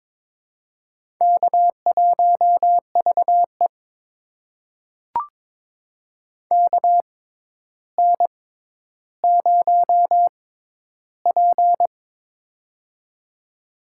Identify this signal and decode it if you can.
Morse code, heard as a keyed tone, about 22 wpm, spaced as for 11 wpm, 700 Hz. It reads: K1VE KN0P
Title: USA Call Signs - 6x Character Spacing - 22wpm